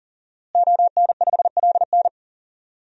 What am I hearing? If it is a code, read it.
Morse code, heard as a keyed tone, about 40 wpm, 700 Hz. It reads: ON5LN
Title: DXCC-100 Call Signs - 5 Characters - Repeated 2x - 40wpm